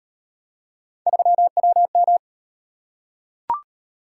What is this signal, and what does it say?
Morse code, heard as a keyed tone, about 37 wpm, 700 Hz. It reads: 3WM